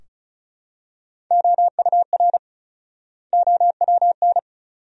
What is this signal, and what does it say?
Morse code, heard as a keyed tone, about 36 wpm, 700 Hz. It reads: OUR OWN
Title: 4x Word-Spacing - Sentences from Top 300 Words - 35wpm